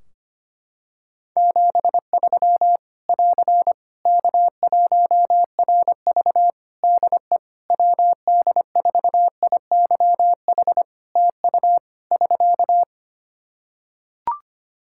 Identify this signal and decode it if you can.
Morse code, heard as a keyed tone, about 25 wpm, 700 Hz. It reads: 73 <AR> K1RV DE WD4IYH TU <SK>
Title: Practice QSO - Exchange 4 - Ending - 25wpm